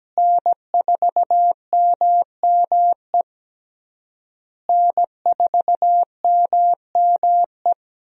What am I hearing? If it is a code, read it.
Morse code, heard as a keyed tone, about 17 wpm, 700 Hz. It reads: N4MME N4MME
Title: USA Call Signs - 5 Characters - Repeated 3x - 17wpm